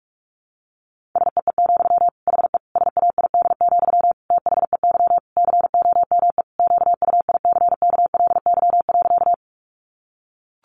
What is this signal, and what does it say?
Morse code, heard as a keyed tone, about 44 wpm, 700 Hz. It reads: SEE, HE SAID, THEY COME QUICKLY.
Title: Sentences from Top 700 Words 45wpm